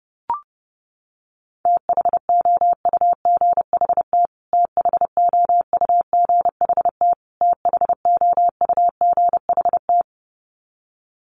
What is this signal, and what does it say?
Morse code, heard as a keyed tone, about 30 wpm, 700 Hz. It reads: THOUGHT THOUGHT THOUGHT
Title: Top 500 Words Repeated 30wpm